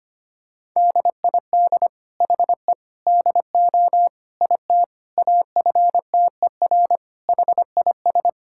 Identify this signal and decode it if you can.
Morse code, heard as a keyed tone, about 25 wpm, 700 Hz. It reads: DID HE DO IT AFTER HIS
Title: Sentences from Top 1600 Words 25wpm